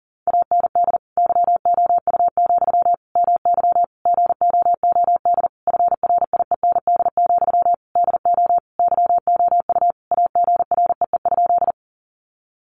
Morse code, heard as a keyed tone, about 40 wpm, 700 Hz. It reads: AND YOU, MY GOOD FRIEND, DO YOU AGREE?
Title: Sentences from Top 1100 Words 40wpm